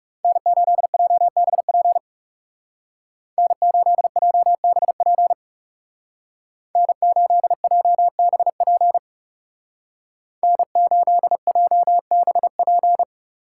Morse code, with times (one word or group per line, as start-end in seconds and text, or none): 0.2-2.0 N8JBP
3.4-5.3 N8JBP
6.8-9.0 N8JBP
10.4-13.0 N8JBP